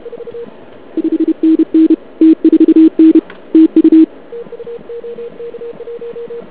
Morse code, heard as a keyed tone, about 31 wpm, 320 Hz. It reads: 5NN T4N TU
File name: ARCHIV / KX3 / CW pileup